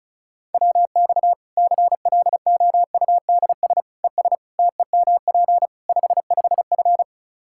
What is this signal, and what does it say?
Morse code, heard as a keyed tone, about 36 wpm, 700 Hz. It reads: WX CLOUDS ES TEMP 55F
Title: Practice QSO - Exchange 2 - Info - 35wpm